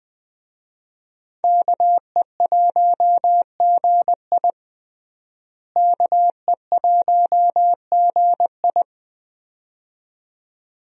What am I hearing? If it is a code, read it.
Morse code, heard as a keyed tone, about 20 wpm, 700 Hz. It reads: KE1GI KE1GI
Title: USA Call Signs - 5 Characters - Repeated 2x - 20wpm